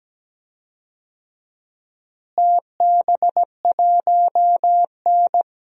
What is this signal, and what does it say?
Morse code, heard as a keyed tone, about 17 wpm, 700 Hz. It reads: TB1N
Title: DXCC-100 Call Signs - 4 Characters - Repeated 2x - 17wpm